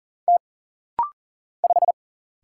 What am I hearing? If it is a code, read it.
Morse code, heard as a keyed tone, about 40 wpm, 700 Hz. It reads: T 5
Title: Single Letter-Number Alt - Rapid-Fire 40wpm